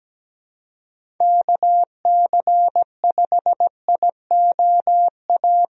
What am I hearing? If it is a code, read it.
Morse code, heard as a keyed tone, about 17 wpm, 700 Hz. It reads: KC5IOA
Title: USA Call Signs - 6 Characters - Repeated 3x - 17wpm